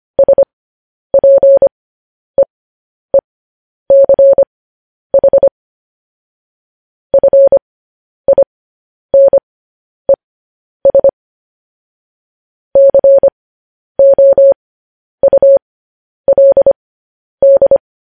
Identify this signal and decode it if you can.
Morse code, heard as a keyed tone, about 25 wpm, spaced as for 10 wpm, 550 Hz. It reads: SPEECH FINES COULD
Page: news-headlines-in-morse-code-at-10-wpm